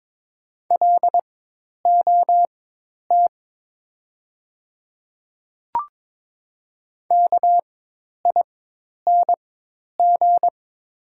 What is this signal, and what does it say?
Morse code, heard as a keyed tone, about 22 wpm, spaced as for 10 wpm, 700 Hz. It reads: LOT KING
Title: 4x Character Spacing - Top 500 Words - 22wpm